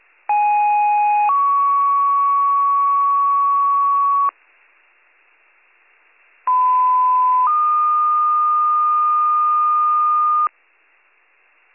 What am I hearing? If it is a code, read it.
Two-tone page, heard as a one-second tone, then a three-second tone, about 830 then 1120 Hz, about 1010 then 1220 Hz. It is heard from 0.3 to 10.5 s.